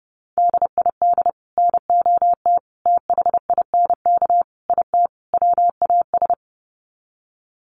Morse code, heard as a keyed tone, about 30 wpm, 700 Hz. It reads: DID NOT THINK IT WAS
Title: Sets of 5 Words Encore 30wpm